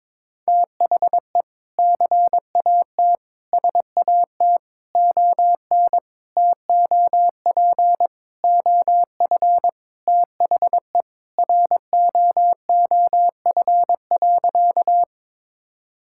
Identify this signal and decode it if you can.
Morse code, heard as a keyed tone, about 22 wpm, 700 Hz. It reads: THE CAT SAT ON TOP OF THE ROOF.
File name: practice / 22wpm / Sentences with 3-Word Phrases - 22wpm